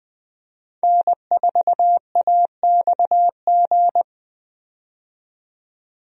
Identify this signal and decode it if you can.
Morse code, heard as a keyed tone, about 20 wpm, 700 Hz. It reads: N4AXG
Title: USA Call Signs - 5 Characters - 20wpm